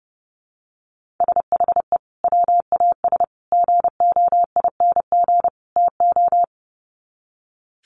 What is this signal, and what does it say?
Morse code, heard as a keyed tone, about 30 wpm, 700 Hz. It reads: SHE WAS GOING TO
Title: Sets of 4 Words 30wpm